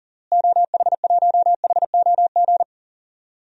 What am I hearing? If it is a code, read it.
Morse code, heard as a keyed tone, about 40 wpm, 700 Hz. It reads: OH1HOG
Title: DXCC-100 Call Signs - 6 Characters - Repeated 2x - 40wpm